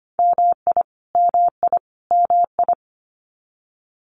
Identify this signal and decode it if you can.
Morse code, heard as a keyed tone, about 25 wpm, 700 Hz. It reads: MI MI MI
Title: US States Repeated 3x 25wpm